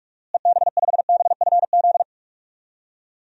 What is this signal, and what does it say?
Morse code, heard as a keyed tone, about 41 wpm, 700 Hz. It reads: EB5BFZ
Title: DXCC-100 Call Signs - 6 Characters - Repeated 2x - 45wpm